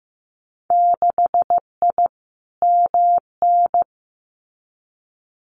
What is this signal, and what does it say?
Morse code, heard as a keyed tone, about 15 wpm, 700 Hz. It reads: 6I MN
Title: Winter Field Day Exchanges Encore 15wpm